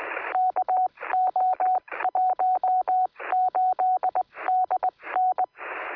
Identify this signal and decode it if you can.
Morse code, heard as a keyed tone, about 20 wpm, spaced as for 17 wpm, 740 Hz, with radio noise. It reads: KO18DN
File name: sm7lcb / 2025z es0shf 23cm